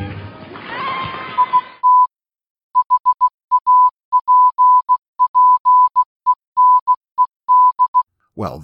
Morse code, heard as a keyed tone, about 16 wpm, 1000 Hz. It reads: IT HAPPENED